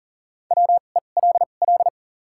Morse code, heard as a keyed tone, about 37 wpm, spaced as for 28 wpm, 700 Hz. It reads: WELL
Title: 2x Character Spacing - Top 500 Words - 40wpm